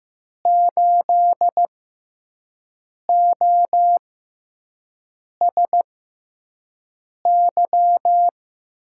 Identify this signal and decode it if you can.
Morse code, heard as a keyed tone, about 15 wpm, spaced as for 5 wpm, 700 Hz. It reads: 8OSY
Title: USA Call Signs - 6 Characters - 6x Character Spacing - 15wpm